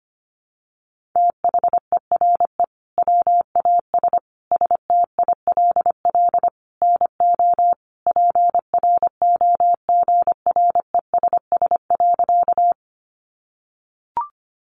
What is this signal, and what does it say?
Morse code, heard as a keyed tone, about 25 wpm, 700 Hz. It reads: THERE WAS STILL NO PROGRESS.
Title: Sentences from Top 1400 Words 25wpm